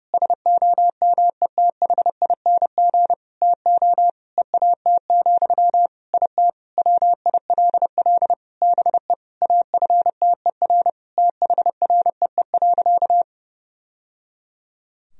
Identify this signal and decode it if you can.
Morse code, heard as a keyed tone, about 30 wpm, 700 Hz. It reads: SOMETHING TO EAT, IT WILL BE AFTER THREE.
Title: Sentences from Top 1000 Words - Review Part 1 - 30wpm